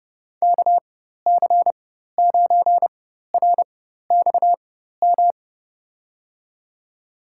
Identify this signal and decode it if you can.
Morse code, heard as a keyed tone, about 30 wpm, spaced as for 14 wpm, 700 Hz. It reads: KC9RXM
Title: USA Call Signs - 6 Characters - 4x Character Spacing - 30wpm